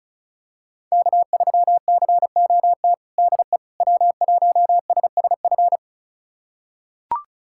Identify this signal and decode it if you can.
Morse code, heard as a keyed tone, about 36 wpm, 700 Hz. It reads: K3COT DE W1SSF
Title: USA Callsign DE Callsign 35wpm